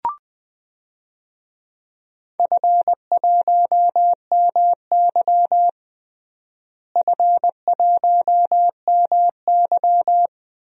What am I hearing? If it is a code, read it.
Morse code, heard as a keyed tone, about 20 wpm, 700 Hz. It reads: F1MY F1MY
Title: DXCC-100 Call Signs - 4 Characters - Repeated 2x - 20wpm